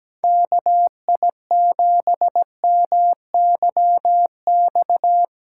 Morse code, heard as a keyed tone, about 17 wpm, 700 Hz. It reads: KI7MYX
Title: USA Call Signs - 8x Character Spacing - 17wpm